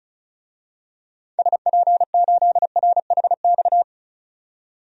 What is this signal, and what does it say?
Morse code, heard as a keyed tone, about 36 wpm, 700 Hz. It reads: SP8RHX
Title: DXCC-100 Call Signs - 6 Characters - 35wpm